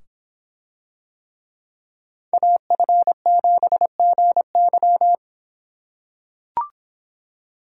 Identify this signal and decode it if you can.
Morse code, heard as a keyed tone, about 26 wpm, 700 Hz. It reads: AF7GY